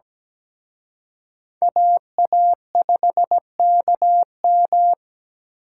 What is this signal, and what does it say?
Morse code, heard as a keyed tone, about 20 wpm, 700 Hz. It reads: AA5KM
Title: Speed Racing - USA 2x2 Call Signs 17wpm